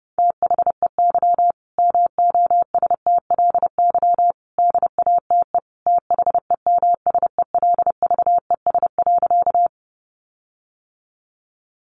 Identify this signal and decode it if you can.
Morse code, heard as a keyed tone, about 30 wpm, 700 Hz. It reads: THEY MOSTLY DATE THEMSELVES.
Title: Sentences from Top 1300 Words 30wpm